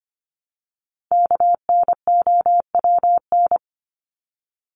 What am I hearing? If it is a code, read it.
Morse code, heard as a keyed tone, about 25 wpm, 700 Hz. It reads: KNOWN
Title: Top 501-750 Words 25wpm